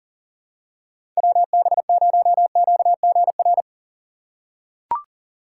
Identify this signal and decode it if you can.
Morse code, heard as a keyed tone, about 40 wpm, 700 Hz. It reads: WB0QGR
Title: USA Call Signs - 6 Characters - 4x Character Spacing - 40wpm